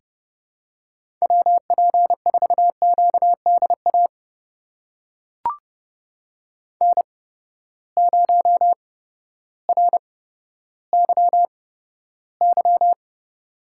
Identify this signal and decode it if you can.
Morse code, heard as a keyed tone, about 30 wpm, spaced as for 14 wpm, 700 Hz. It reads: WP4QDA N0RYY
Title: USA Call Signs - 8x Character Spacing - 30wpm